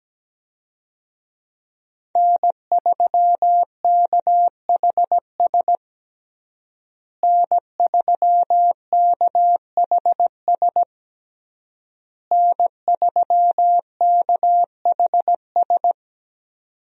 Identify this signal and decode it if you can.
Morse code, heard as a keyed tone, about 17 wpm, 700 Hz. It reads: N3KHS N3KHS N3KHS
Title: USA Call Signs - 5 Characters - Repeated 3x - 17wpm